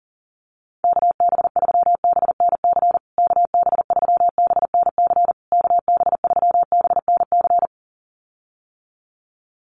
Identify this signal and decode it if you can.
Morse code, heard as a keyed tone, about 40 wpm, 700 Hz. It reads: KB3BNC KB3BNC KB3BNC
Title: USA Call Signs Repeated 3x 40wpm